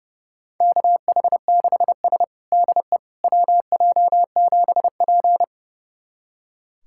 Morse code, heard as a keyed tone, about 32 wpm, 700 Hz. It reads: KH6S DE WJ7P
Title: USA Callsign DE Callsign - 4 Characters - 30wpm